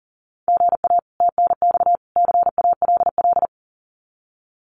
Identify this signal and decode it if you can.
Morse code, heard as a keyed tone, about 40 wpm, 700 Hz. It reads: GA TNX CALL
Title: Common QSO Phrases 40wpm